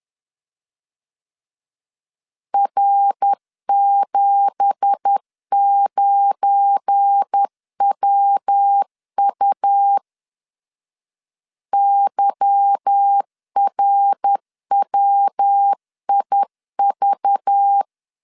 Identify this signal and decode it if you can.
Morse code, heard as a keyed tone, about 10 wpm, 800 Hz. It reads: R79WU YRWIV
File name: CW / DET / 1 9 1500C S C 10wpm 4557